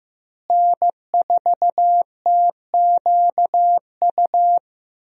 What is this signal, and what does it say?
Morse code, heard as a keyed tone, about 15 wpm, 700 Hz. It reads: N4TQU
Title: USA Call Signs - 5 Characters - Repeated 2x - 15wpm